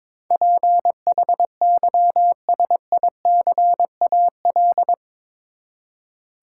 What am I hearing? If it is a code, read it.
Morse code, heard as a keyed tone, about 22 wpm, 700 Hz. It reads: PHYSICAL
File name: practice / 22wpm / Eight Letter Words - 22wpm